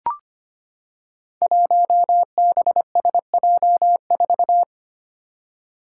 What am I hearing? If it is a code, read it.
Morse code, heard as a keyed tone, about 25 wpm, 700 Hz. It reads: 1BSJ4
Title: Five Challenging Letter-Number Combinations 25wpm